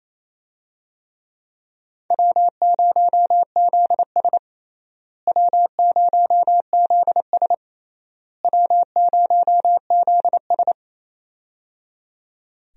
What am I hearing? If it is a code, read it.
Morse code, heard as a keyed tone, about 28 wpm, 700 Hz. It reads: W0ZS W0ZS W0ZS
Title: USA Call Signs - 4 Characters - Repeated 3x - 28wpm